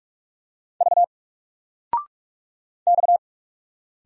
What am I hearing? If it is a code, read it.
Morse code, heard as a keyed tone, about 46 wpm, 700 Hz. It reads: V X
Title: Single Letter-Number 45wpm